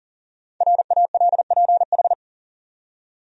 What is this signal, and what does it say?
Morse code, heard as a keyed tone, about 39 wpm, 700 Hz. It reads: RALPH
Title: USA Common Names - Max 5 Letters - 40wpm